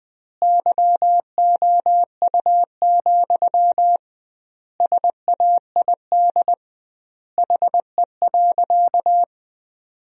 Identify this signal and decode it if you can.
Morse code, heard as a keyed tone, about 20 wpm, 700 Hz. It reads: YOU, SAID HE.
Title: 2x Word-Spacing - Sentences from Top 400 Words - 20wpm